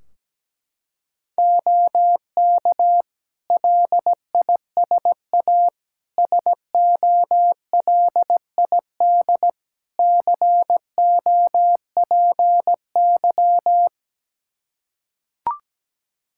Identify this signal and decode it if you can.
Morse code, heard as a keyed tone, about 17 wpm, 700 Hz. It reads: OK LISA SOLID COPY
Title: Practice QSO - Exchange 3 - More Info - 17wpm